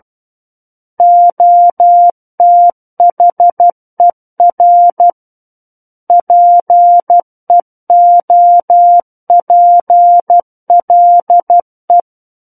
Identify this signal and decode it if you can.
Morse code, heard as a keyed tone, about 12 wpm, 700 Hz. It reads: OTHER PEOPLE
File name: Book One / 03-Chapter 3 - 12 WPM